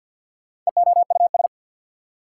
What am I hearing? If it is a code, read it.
Morse code, heard as a keyed tone, about 61 wpm, 690 Hz. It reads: EOUS
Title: Speed-Racing Prefixes and Suffixes 50wpm